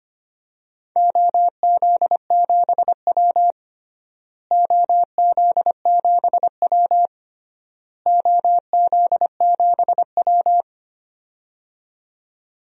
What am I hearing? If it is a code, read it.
Morse code, heard as a keyed tone, about 25 wpm, 700 Hz. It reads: OZ7W OZ7W OZ7W
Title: DXCC-100 Call Signs - 4 Characters - Repeated 3x - 25wpm